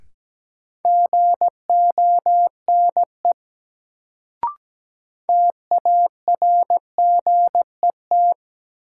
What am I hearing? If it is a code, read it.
Morse code, heard as a keyed tone, about 17 wpm, 700 Hz. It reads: GONE TARGET
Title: Sentences from Top 1200 Words - New Word Drill - 17wpm